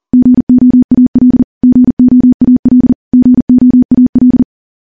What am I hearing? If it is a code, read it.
Morse code, heard as a keyed tone, about 40 wpm, 250 Hz. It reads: GOAL GOAL GOAL
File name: morse / morse-training-40WPM